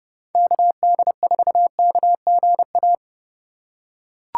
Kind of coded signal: Morse code

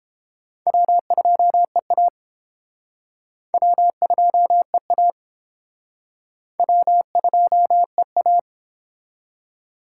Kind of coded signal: Morse code

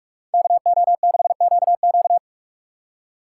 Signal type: Morse code

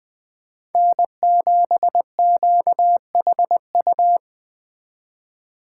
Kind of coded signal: Morse code

